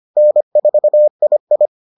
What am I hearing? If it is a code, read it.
Morse code, heard as a keyed tone, about 25 wpm, 600 Hz. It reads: N4II